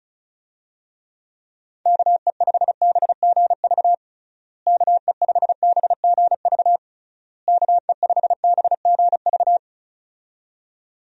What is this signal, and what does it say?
Morse code, heard as a keyed tone, about 36 wpm, 700 Hz. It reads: KE5BGV KE5BGV KE5BGV
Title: USA Call Signs - 6 Characters - Repeated 3x - 35wpm